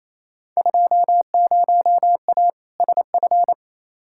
Morse code, heard as a keyed tone, about 28 wpm, 700 Hz. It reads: 20A SF
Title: Field Day Exchanges Encore - 28wpm